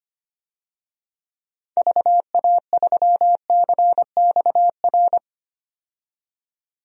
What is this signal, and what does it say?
Morse code, heard as a keyed tone, about 25 wpm, 700 Hz. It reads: VA3CXR
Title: DXCC-100 Call Signs - 6 Characters - 25wpm